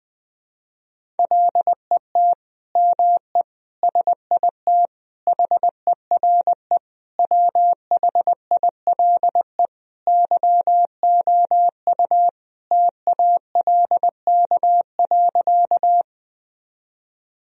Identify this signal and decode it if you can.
Morse code, heard as a keyed tone, about 20 wpm, 700 Hz. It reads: LET ME SIT HERE WHILE YOU TALK.
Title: Sentences from Top 1000 Words - Review Part 1 - 20wpm